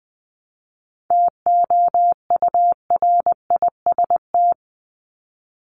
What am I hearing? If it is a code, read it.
Morse code, heard as a keyed tone, about 20 wpm, 700 Hz. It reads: TOURIST
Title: Top 1501-1750 Words 20wpm